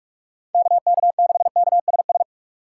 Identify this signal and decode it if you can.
Morse code, heard as a keyed tone, about 41 wpm, 700 Hz. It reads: KK6KSS
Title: USA Call Signs - 6 Characters - 6x Character Spacing - 45wpm